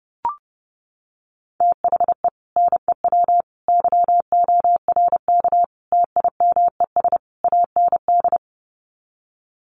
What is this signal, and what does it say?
Morse code, heard as a keyed tone, about 30 wpm, 700 Hz. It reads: THE NEW YORK TIMES AND
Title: Sets of 5 Words Encore 30wpm